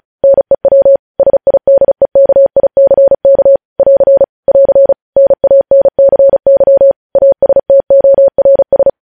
Morse code, heard as a keyed tone, about 36 wpm, 550 Hz. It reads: NEW SIDEKICK <AR> <AR> NANCY ASTORS
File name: podcast / News headlines-1x-35x35